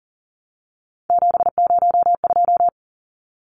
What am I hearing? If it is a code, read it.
Morse code, heard as a keyed tone, about 40 wpm, 700 Hz. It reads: 702
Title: Numbers - 3-Digit 40wpm